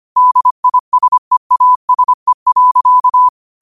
Morse code, heard as a keyed tone, about 25 wpm, 1000 Hz. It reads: DISEASE.
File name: qotdfiles / 25wpm